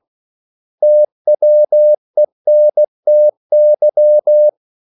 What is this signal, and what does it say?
Morse code, heard as a keyed tone, about 16 wpm, 600 Hz. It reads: TWENTY
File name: CW / Practice / 500MostCommonEnglishWords16